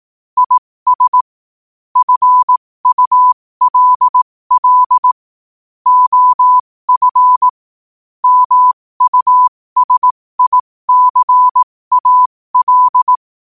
Morse code, heard as a keyed tone, about 18 wpm, spaced as for 15 wpm, 1000 Hz. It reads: IS FULL OF MUSICAL
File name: qotdfiles / 15wpm